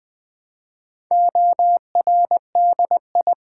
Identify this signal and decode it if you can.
Morse code, heard as a keyed tone, about 20 wpm, 700 Hz. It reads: ORDI
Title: Build A Word - 2 Letters At A Time - 8-Letter Words - 20wpm